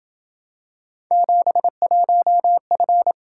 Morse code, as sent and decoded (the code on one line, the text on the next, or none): --... .---- ..-.
71F